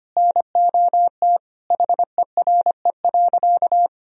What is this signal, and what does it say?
Morse code, heard as a keyed tone, about 25 wpm, 700 Hz. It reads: NOT HERE.
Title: Sentences from Top 1000 Words - Review Part 1 - 25wpm